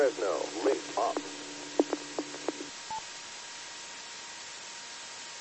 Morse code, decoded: none

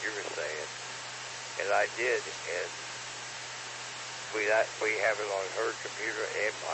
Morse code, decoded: none

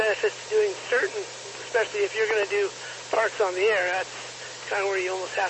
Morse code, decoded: none

